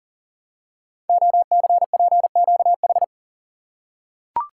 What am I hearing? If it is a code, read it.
Morse code, heard as a keyed tone, about 39 wpm, 700 Hz. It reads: OCPQH
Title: Code Groups - 5 Letter:Numbers - 40wpm